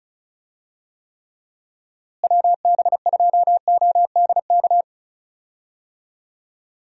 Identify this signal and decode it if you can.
Morse code, heard as a keyed tone, about 36 wpm, 700 Hz. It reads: WB2ODK
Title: USA Call Signs - 6 Characters - Repeated 2x - 35wpm